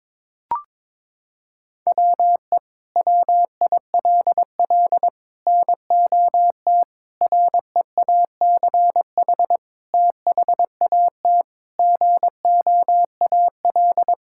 Morse code, heard as a keyed tone, about 22 wpm, 700 Hz. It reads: WE WILL NOT REACH THAT GOAL
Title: Sentences from Top 1000 Words - Review Part 5 - 22wpm